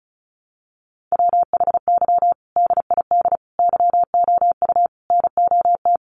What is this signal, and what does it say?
Morse code, heard as a keyed tone, about 34 wpm, 700 Hz. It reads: WHY DID YOU NOT
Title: Sentences from Top 1200 Words 35wpm